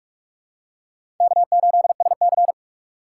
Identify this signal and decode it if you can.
Morse code, heard as a keyed tone, about 47 wpm, 700 Hz. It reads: K8SC
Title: USA Call Signs - 6x Character Spacing - 45wpm